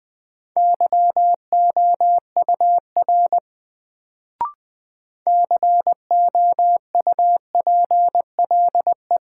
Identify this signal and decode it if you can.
Morse code, heard as a keyed tone, about 20 wpm, 700 Hz. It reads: YOUR COUPLE
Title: Sets of 2 Words - New Word Drill - 20wpm